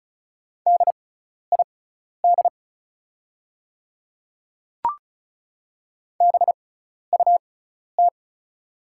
Morse code, heard as a keyed tone, about 36 wpm, spaced as for 12 wpm, 700 Hz. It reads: DID BUT
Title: 6x Character Spacing - Top 100 Words - 35wpm